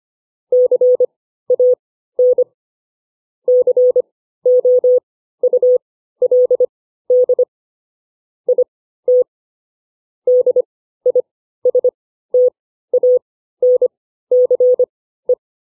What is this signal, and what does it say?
Morse code, heard as a keyed tone, about 25 wpm, spaced as for 14 wpm, 500 Hz. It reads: CAN COULD IT DISTANCE
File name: CW / farnsworth timing